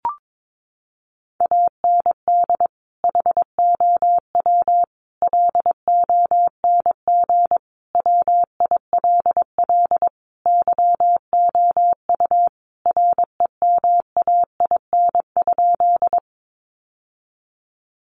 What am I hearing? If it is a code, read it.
Morse code, heard as a keyed tone, about 22 wpm, 700 Hz. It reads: AND HOW LONG WILL YOU REMAIN?
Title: Sentences from Top 1200 Words 22wpm